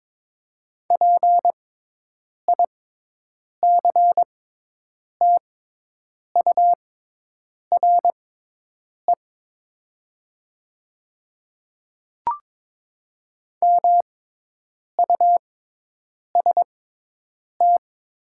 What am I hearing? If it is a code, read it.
Morse code, heard as a keyed tone, about 22 wpm, spaced as for 8 wpm, 700 Hz. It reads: PICTURE MUST
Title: 6x Character Spacing - Top 200 Words - 22wpm